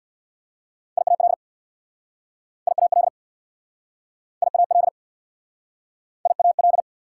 Morse code, heard as a keyed tone, about 60 wpm, 690 Hz. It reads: IAL IAL IAL IAL
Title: Speed-Racing Prefixes and Suffixes 50wpm